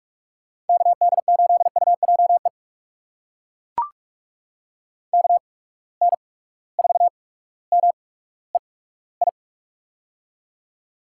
Morse code, heard as a keyed tone, about 42 wpm, spaced as for 19 wpm, 700 Hz. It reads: KD8UJE KN4MEI
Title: USA Call Signs - 6 Characters - 8x Character Spacing - 45wpm